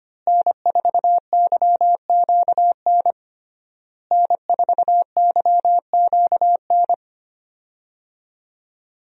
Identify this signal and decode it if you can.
Morse code, heard as a keyed tone, about 25 wpm, 700 Hz. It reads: N4YQN N4YQN
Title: USA Call Signs - 5 Characters - Repeated 3x - 25wpm